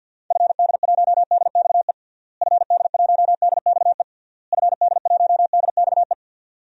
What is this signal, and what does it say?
Morse code, heard as a keyed tone, about 52 wpm, 700 Hz. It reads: FD1DXE FD1DXE FD1DXE
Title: DXCC-100 Call Signs - 6 Characters - Repeated 3x - 50wpm